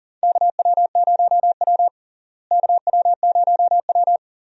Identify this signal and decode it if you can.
Morse code, heard as a keyed tone, about 40 wpm, 700 Hz. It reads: KW0W KW0W
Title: USA Call Signs - 4 Characters - Repeated 2x - 40wpm